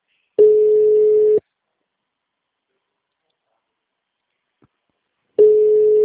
Ringback tone (Europe, 425 Hz).
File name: out- -2035- - - .48